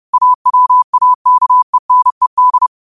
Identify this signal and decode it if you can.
Morse code, heard as a keyed tone, about 30 wpm, 1000 Hz. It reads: AWAKENED
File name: qotdfiles / 30wpm